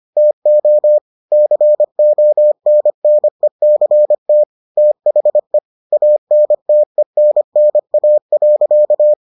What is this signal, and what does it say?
Morse code, heard as a keyed tone, about 25 wpm, 600 Hz. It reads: TO CONNECT THE ANTENNA.